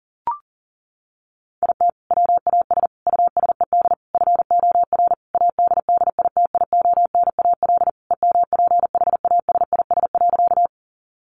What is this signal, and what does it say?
Morse code, heard as a keyed tone, about 40 wpm, 700 Hz. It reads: IT WAS USED FOR ADDITIONAL EMPHASIS.
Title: Sentences from Top 1400 Words 40wpm